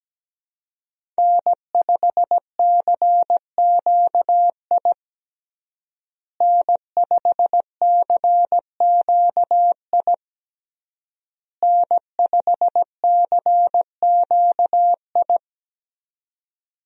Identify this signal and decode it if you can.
Morse code, heard as a keyed tone, about 17 wpm, 700 Hz. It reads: N5CQI N5CQI N5CQI
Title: USA Call Signs - 5 Characters - Repeated 3x - 17wpm